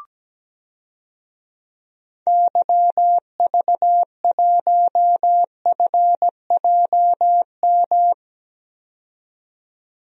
Morse code, heard as a keyed tone, about 17 wpm, 700 Hz. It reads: YV1FJM
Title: DXCC-100 Call Signs - 6 Characters - 17wpm